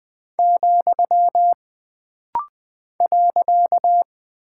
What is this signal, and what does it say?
Morse code, heard as a keyed tone, about 20 wpm, 700 Hz. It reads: , .
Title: Single Punctuation-Prosign - Rapid-Fire - 20wpm